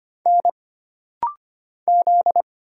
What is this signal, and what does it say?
Morse code, heard as a keyed tone, about 25 wpm, 700 Hz. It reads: N Z
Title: Single Letters - Rapid-Fire 25wpm